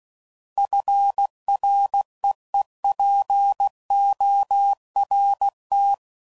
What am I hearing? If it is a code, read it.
Morse code, heard as a keyed tone, about 16 wpm, 800 Hz. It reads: FREEPORT
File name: CW Code Groups @15WPM / fromSM's / Cities